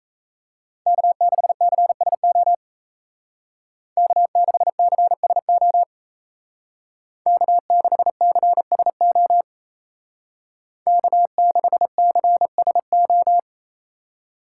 Morse code, heard as a keyed tone, about 33 wpm, 700 Hz. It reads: K6CSO K6CSO K6CSO K6CSO
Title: Speed Racing - USA 1x3 Call Signs 28wpm